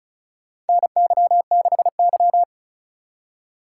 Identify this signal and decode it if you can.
Morse code, heard as a keyed tone, about 36 wpm, 700 Hz. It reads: NY6Y
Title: K1USN SST QSO Practice - 35wpm